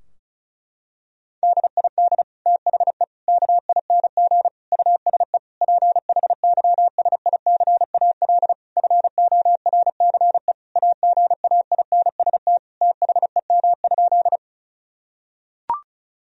Morse code, heard as a keyed tone, about 34 wpm, 700 Hz. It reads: DID THE KING USE PHYSICAL FORCE AGAINST THEM?
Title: Sentences from Top 1000 Words - Review Part 4 - 35wpm